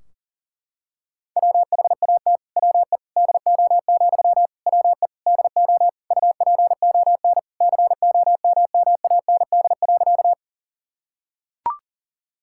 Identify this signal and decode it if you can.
Morse code, heard as a keyed tone, about 40 wpm, 700 Hz. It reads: WHAT WE DO, WE DO UPON COMMAND.
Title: Sentences from Top 1600 Words 40wpm